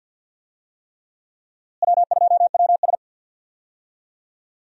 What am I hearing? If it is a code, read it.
Morse code, heard as a keyed tone, about 46 wpm, 700 Hz. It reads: W2WS
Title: USA Call Signs - 4 Characters - Repeated 3x - 50wpm